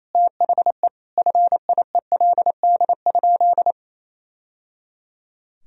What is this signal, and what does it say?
Morse code, heard as a keyed tone, about 28 wpm, 700 Hz. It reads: THE FIELD?
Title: Sentences from Top 1000 Words - Review Part 2 - 28wpm